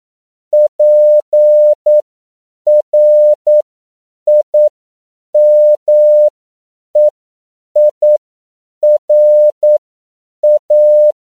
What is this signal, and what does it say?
Morse code, heard as a keyed tone, about 9 wpm, spaced as for 7 wpm, 600 Hz. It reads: PRIMEIRA
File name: CW / 10PPM